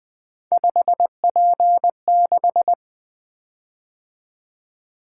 Morse code, heard as a keyed tone, about 20 wpm, 700 Hz. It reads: 5P6
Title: Three Letter-Number Combinations - 20wpm